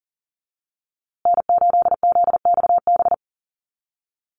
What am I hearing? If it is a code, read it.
Morse code, heard as a keyed tone, about 40 wpm, 700 Hz. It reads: N8ZXB
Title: USA Call Signs Repeated 3x 40wpm